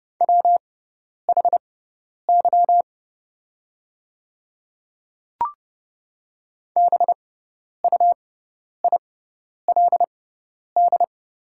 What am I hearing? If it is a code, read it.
Morse code, heard as a keyed tone, about 30 wpm, spaced as for 10 wpm, 700 Hz. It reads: WHY BUILD
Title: 6x Character Spacing - Top 300 Words - 30wpm